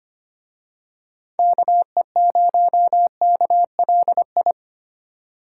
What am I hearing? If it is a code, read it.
Morse code, heard as a keyed tone, about 27 wpm, 700 Hz. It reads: KE0KLI
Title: Speed Racing - USA 2x3 Call Signs 25wpm